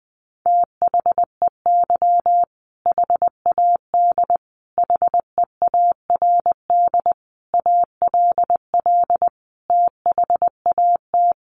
Morse code, heard as a keyed tone, about 20 wpm, 700 Hz. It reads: THEY HAD HEARD ALL THAT
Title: Sentences from Top 400 Words 20wpm